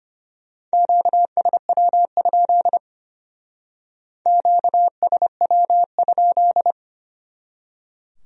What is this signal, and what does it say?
Morse code, heard as a keyed tone, about 32 wpm, 700 Hz. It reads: QSW? QSW?